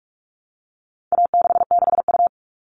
Morse code, heard as a keyed tone, about 44 wpm, 700 Hz. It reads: A66U
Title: All Cut Numbers - 4-Digit 45wpm